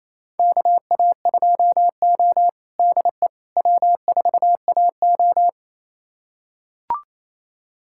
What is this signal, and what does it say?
Morse code, heard as a keyed tone, about 27 wpm, 690 Hz. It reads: KA2O DE W4AO
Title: USA Callsign DE Callsign - 4 Characters - 28wpm